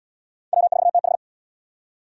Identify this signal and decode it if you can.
Morse code, heard as a keyed tone, about 75 wpm, 700 Hz. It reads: W4TH